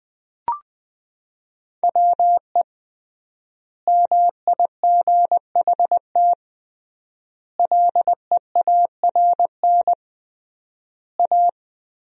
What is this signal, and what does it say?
Morse code, heard as a keyed tone, about 20 wpm, 700 Hz. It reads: WE MIGHT LEARN A
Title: 3x Word-Spacing - Sentences from Top 300 Words - 20wpm